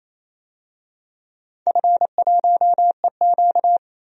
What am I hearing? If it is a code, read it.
Morse code, heard as a keyed tone, about 28 wpm, 700 Hz. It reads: F1EQ
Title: DXCC-100 Call Signs - 4 Characters - 28wpm